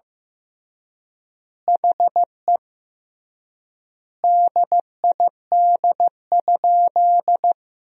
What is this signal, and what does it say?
Morse code, heard as a keyed tone, about 15 wpm, 700 Hz. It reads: HE DID?
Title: 3x Word-Spacing - Sentences from Top 100 Words - 15wpm